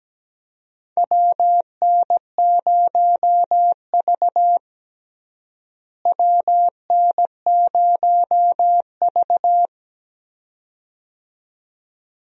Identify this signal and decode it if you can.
Morse code, heard as a keyed tone, about 17 wpm, 700 Hz. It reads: WN0V WN0V
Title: USA Call Signs - 4 Characters - Repeated 2x - 17wpm